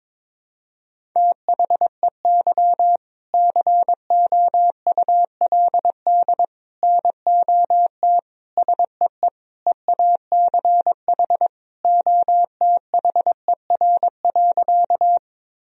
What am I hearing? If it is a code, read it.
Morse code, heard as a keyed tone, about 22 wpm, 700 Hz. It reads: THEY COULD NOT SEE EACH OTHER.
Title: Sentences from Top 1000 Words - Review Part 1 - 22wpm